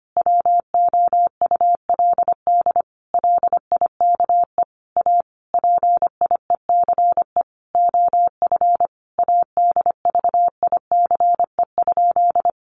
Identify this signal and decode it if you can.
Morse code, heard as a keyed tone, about 25 wpm, 700 Hz. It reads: WOULD LIKE A PIECE OF ADVICE?
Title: Sentences from Top 1100 Words 25wpm